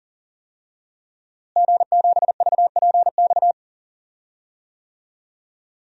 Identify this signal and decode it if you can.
Morse code, heard as a keyed tone, about 40 wpm, 700 Hz. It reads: G7VPX
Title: DXCC-100 Call Signs - 5 Characters - Repeated 3x - 40wpm